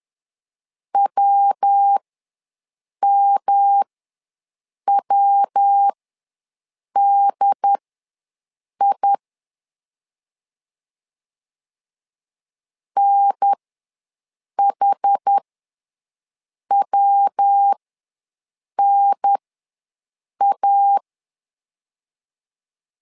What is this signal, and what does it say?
Morse code, heard as a keyed tone, about 10 wpm, spaced as for 6 wpm, 800 Hz. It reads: WMWDI NHWNA
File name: LEZIONI / 3 MWRO / 3 MW ws 10wpm 2213